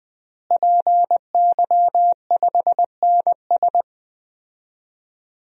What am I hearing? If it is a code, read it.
Morse code, heard as a keyed tone, about 21 wpm, 700 Hz. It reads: PY5NS
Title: DXCC-100 Call Signs - 5 Characters - Repeated 3x - 20wpm